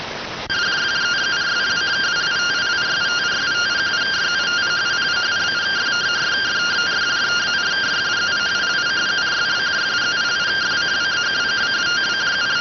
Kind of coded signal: two-tone teleprinter signal

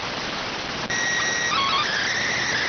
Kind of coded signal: SSTV (slow-scan television)

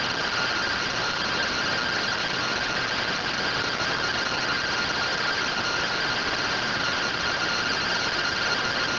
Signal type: two-tone teleprinter signal

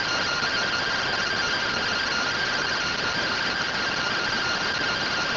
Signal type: two-tone teleprinter signal

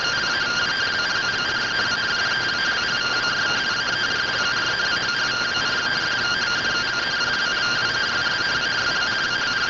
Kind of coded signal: two-tone teleprinter signal

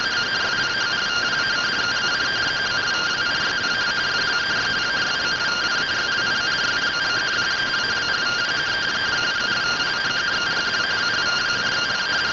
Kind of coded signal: two-tone teleprinter signal

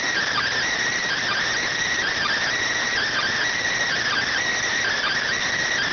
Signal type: Emergency Alert System (SAME) tones

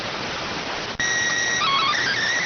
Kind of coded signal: SSTV (slow-scan television)

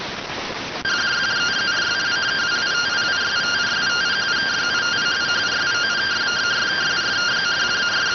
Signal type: two-tone teleprinter signal

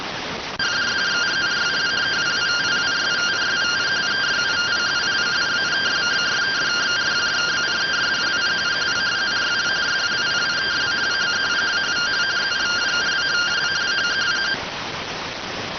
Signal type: two-tone teleprinter signal

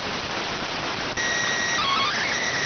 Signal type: SSTV (slow-scan television)

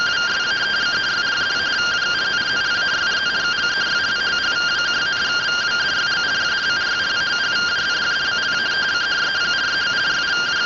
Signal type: two-tone teleprinter signal

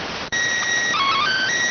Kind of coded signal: SSTV (slow-scan television)

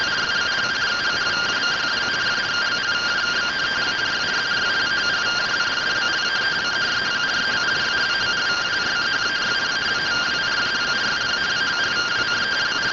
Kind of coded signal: two-tone teleprinter signal

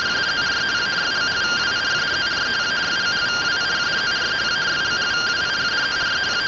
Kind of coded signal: two-tone teleprinter signal

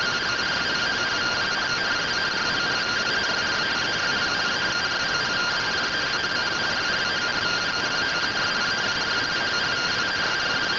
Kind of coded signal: two-tone teleprinter signal

